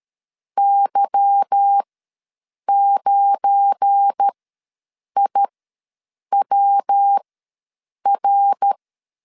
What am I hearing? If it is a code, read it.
Morse code, heard as a keyed tone, about 13 wpm, spaced as for 7 wpm, 800 Hz. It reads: Y9IWR
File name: LEZIONI / 8 YJ45 / 8 YJ45 ws 12wpm 3146